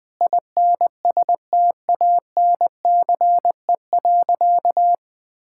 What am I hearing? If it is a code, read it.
Morse code, heard as a keyed tone, about 20 wpm, 700 Hz. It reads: INSTANCE.
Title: Sentences from Top 1000 Words - Review Part 5 - 20wpm